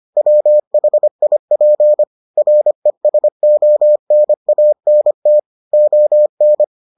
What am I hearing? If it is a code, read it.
Morse code, heard as a keyed tone, about 25 wpm, 600 Hz. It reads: WHIP RESONANT ON